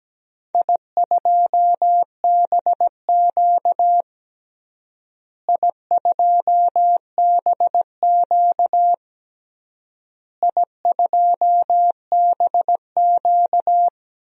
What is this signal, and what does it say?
Morse code, heard as a keyed tone, about 17 wpm, 700 Hz. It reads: I2BQ I2BQ I2BQ
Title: DXCC-100 Call Signs - 4 Characters - Repeated 3x - 17wpm